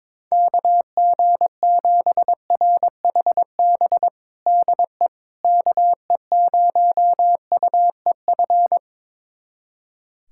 Morse code, heard as a keyed tone, about 22 wpm, 700 Hz. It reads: KG7RHB DE KE0UEF
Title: USA Callsign DE Callsign - 6 Characters - 22wpm